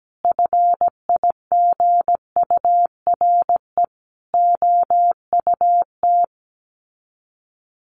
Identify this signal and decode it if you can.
Morse code, heard as a keyed tone, about 17 wpm, 700 Hz. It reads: FIGURE OUT
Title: Sets of 4 Words 17wpm